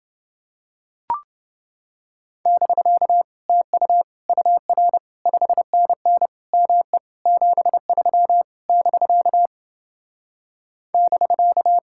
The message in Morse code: <BK> TU UR 5NN ME 73 <BK> <BK>

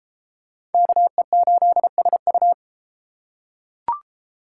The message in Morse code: KE8SU

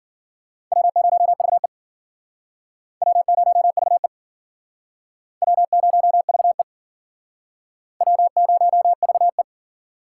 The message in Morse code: W0VE W0VE W0VE W0VE

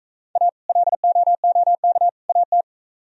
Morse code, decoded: A LOOK AT

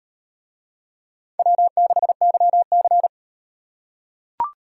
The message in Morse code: W6YC

